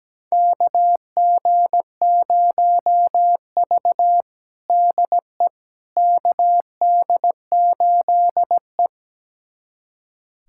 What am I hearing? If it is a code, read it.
Morse code, heard as a keyed tone, about 17 wpm, 700 Hz. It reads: KG0V DE KD8E
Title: USA Callsign DE Callsign - 4 Characters - 17wpm